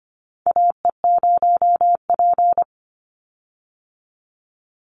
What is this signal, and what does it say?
Morse code, heard as a keyed tone, about 25 wpm, 700 Hz. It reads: AE0P
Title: Call Signs - 25wpm